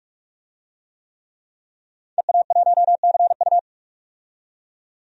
Morse code, heard as a keyed tone, about 46 wpm, 700 Hz. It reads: EA1CU
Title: DXCC-100 Call Signs - 5 Characters - 45wpm